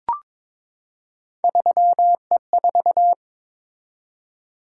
Morse code, heard as a keyed tone, about 22 wpm, 700 Hz. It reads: 3E4
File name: practice / 22wpm / Three Letter-Number Combinations - 22wpm